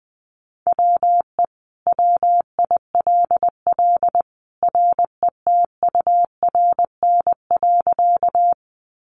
Morse code, heard as a keyed tone, about 20 wpm, 700 Hz. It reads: WE WILL RETURN.
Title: Sentences from Top 800 Words 20wpm